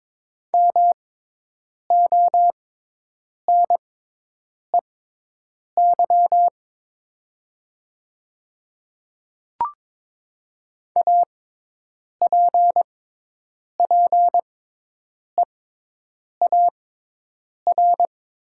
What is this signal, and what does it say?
Morse code, heard as a keyed tone, about 21 wpm, spaced as for 8 wpm, 700 Hz. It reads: MONEY APPEAR
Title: 6x Character Spacing - Top 500 Words - 22wpm